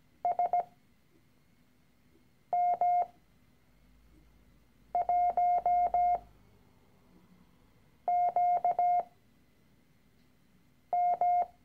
Morse code, heard as a keyed tone, about 17 wpm, spaced as for 4 wpm, 710 Hz. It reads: SM1QM
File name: CW-Online-Kurs / Anfänger / Lektion 013